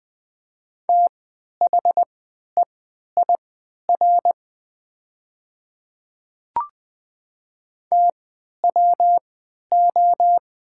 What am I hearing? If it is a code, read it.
Morse code, heard as a keyed tone, about 20 wpm, spaced as for 11 wpm, 700 Hz. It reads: THEIR TWO
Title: 3x Character Spacing - Top 100 Words - 20wpm